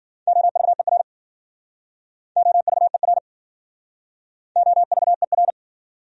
Morse code, heard as a keyed tone, about 52 wpm, 700 Hz. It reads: OVER OVER OVER